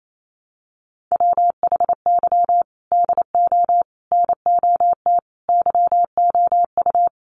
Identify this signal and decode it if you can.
Morse code, heard as a keyed tone, about 28 wpm, 700 Hz. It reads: WHY DO NOT YOU
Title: Sets of 5 Words 28wpm